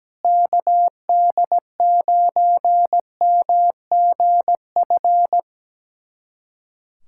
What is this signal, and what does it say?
Morse code, heard as a keyed tone, about 17 wpm, 700 Hz. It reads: KD9MGF
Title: USA Callsign DE Callsign - 6 Characters - 17wpm